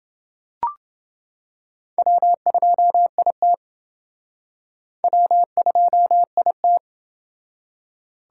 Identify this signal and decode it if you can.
Morse code, heard as a keyed tone, about 28 wpm, 700 Hz. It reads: W2IT W2IT